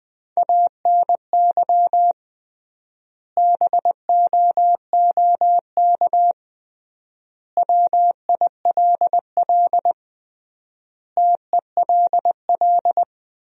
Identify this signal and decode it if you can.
Morse code, heard as a keyed tone, about 20 wpm, 700 Hz. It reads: ANY BOOK WILL TELL
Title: 3x Word-Spacing - Sentences from Top 300 Words - 20wpm